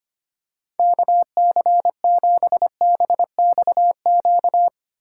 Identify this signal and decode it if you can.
Morse code, heard as a keyed tone, about 25 wpm, 700 Hz. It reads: KC7BXQ
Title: USA Callsign DE Callsign 25wpm